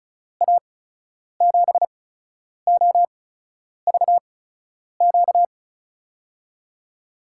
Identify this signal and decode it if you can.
Morse code, heard as a keyed tone, about 36 wpm, spaced as for 10 wpm, 700 Hz. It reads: A7OVQ
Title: USA Call Signs - 8x Character Spacing - 35wpm